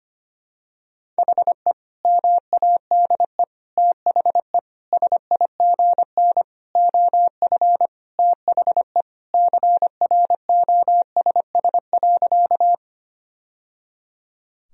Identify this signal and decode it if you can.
Morse code, heard as a keyed tone, about 25 wpm, 700 Hz. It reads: HE MADE THE SIGN OF THE CROSS.
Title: Sentences from Top 1000 Words - Review Part 1 - 25wpm